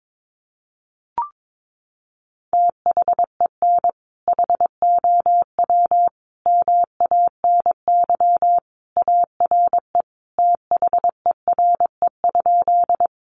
Morse code, heard as a keyed tone, about 22 wpm, 700 Hz. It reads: THEN HOW MANY ARE THERE?
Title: Sentences from Top 100 Words 22wpm